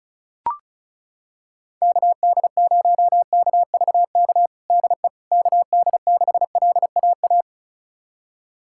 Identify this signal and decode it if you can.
Morse code, heard as a keyed tone, about 34 wpm, 700 Hz. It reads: KD0KVK DE KD6LAA
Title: USA Callsign DE Callsign - 6 Characters - 35wpm